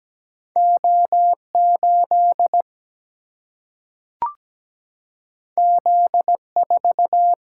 Morse code, heard as a keyed tone, about 17 wpm, 700 Hz. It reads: O 8 Z 4
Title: Single-Letter Single-Number 17wpm